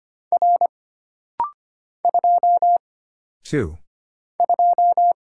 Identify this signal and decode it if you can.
Morse code, heard as a keyed tone, about 25 wpm, 700 Hz. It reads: R 2 2
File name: Single Character (Letter:Number:Punctuation:Prosign) - Rapid-Fire - 25wpm